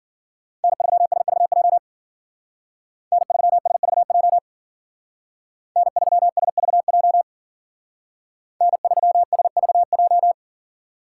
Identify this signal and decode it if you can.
Morse code, heard as a keyed tone, about 52 wpm, 700 Hz. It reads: N3SVJ N3SVJ N3SVJ N3SVJ